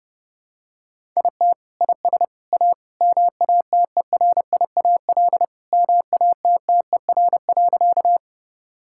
Morse code, heard as a keyed tone, about 30 wpm, 700 Hz. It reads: IT IS A MATERIAL MATTER.
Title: Sentences from Top 1000 Words - Review Part 5 - 30wpm